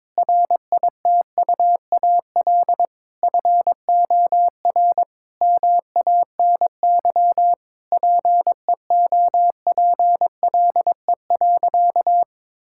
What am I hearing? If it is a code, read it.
Morse code, heard as a keyed tone, about 22 wpm, 700 Hz. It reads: RITUAL FOR MANY PEOPLE.
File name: Difficult Sentences with Hints - 22wpm